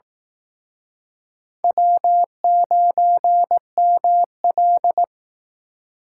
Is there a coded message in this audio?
Yes: Morse code from 1.6 to 5.0 s, heard as a keyed tone, about 20 wpm, 700 Hz. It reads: W9ML